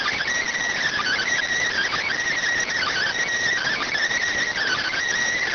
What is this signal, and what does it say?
Emergency Alert System (SAME) tones, heard as harsh buzzing bursts. It is heard from 1.0 to 5.0 s.